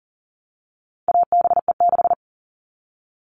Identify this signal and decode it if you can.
Morse code, heard as a keyed tone, about 40 wpm, 700 Hz. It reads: ABE6
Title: All Cut Numbers - 4-Digit 40wpm